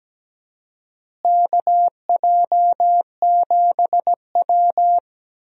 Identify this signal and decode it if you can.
Morse code, heard as a keyed tone, about 17 wpm, 700 Hz. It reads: KJ7W
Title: USA Call Signs - 4 Characters - Repeated 2x - 17wpm